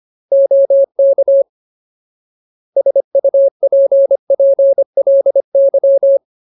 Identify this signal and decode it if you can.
Morse code, heard as a keyed tone, about 25 wpm, 550 Hz. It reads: OK SUPPLY
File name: Ultimate Word List / 25WPM Ultimate